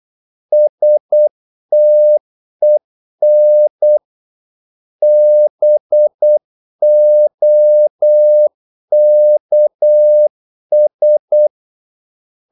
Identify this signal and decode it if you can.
Morse code, heard as a keyed tone, about 8 wpm, 600 Hz. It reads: STEN BOKS
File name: Korte ord DK 8wpm